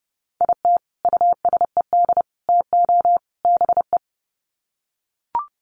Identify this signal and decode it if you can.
Morse code, heard as a keyed tone, about 30 wpm, 700 Hz. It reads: IT USED TO BE
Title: Sets of 4 Words 30wpm